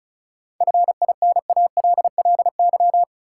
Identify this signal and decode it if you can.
Morse code, heard as a keyed tone, about 36 wpm, 700 Hz. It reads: FINALLY
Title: Sentences from Top 1000 Words - New Word Drill - 35wpm